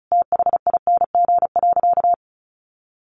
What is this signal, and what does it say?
Morse code, heard as a keyed tone, about 34 wpm, 700 Hz. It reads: THING.
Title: Sentences from Top 200 Words 35wpm